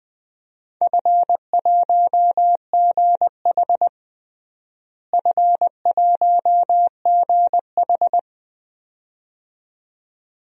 Morse code, heard as a keyed tone, about 21 wpm, 700 Hz. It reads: F1GH F1GH
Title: DXCC-100 Call Signs - 4 Characters - Repeated 2x - 20wpm